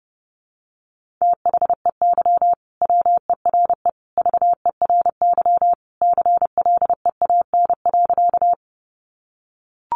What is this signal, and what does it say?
Morse code, heard as a keyed tone, about 30 wpm, 700 Hz. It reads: THEY WERE VERY CLEAN.
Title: Sentences from Top 1100 Words 30wpm